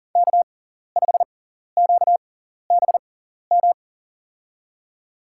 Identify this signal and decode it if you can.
Morse code, heard as a keyed tone, about 40 wpm, spaced as for 14 wpm, 700 Hz. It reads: K5QBM
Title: USA Call Signs - 5 Characters - 6x Character Spacing - 40wpm